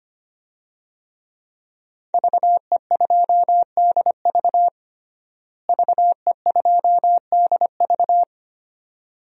Morse code, heard as a keyed tone, about 25 wpm, 700 Hz. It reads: VE2DV VE2DV
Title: DXCC-100 Call Signs - 5 Characters - Repeated 2x - 25wpm